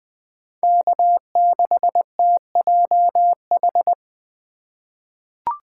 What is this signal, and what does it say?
Morse code, heard as a keyed tone, about 20 wpm, 700 Hz. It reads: K6TJH
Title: USA Call Signs - 5 Characters - 3x Character Spacing - 20wpm